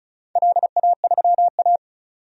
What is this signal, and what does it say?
Morse code, heard as a keyed tone, about 34 wpm, 700 Hz. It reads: LA3A
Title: DXCC-100 Call Signs - 4 Characters - 35wpm